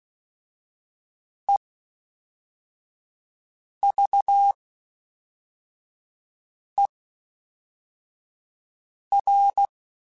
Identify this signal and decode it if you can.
Morse code, heard as a keyed tone, about 16 wpm, spaced as for 4 wpm, 800 Hz. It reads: EVER
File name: Special INGs / Special INGs-mp3s / Fast ing @10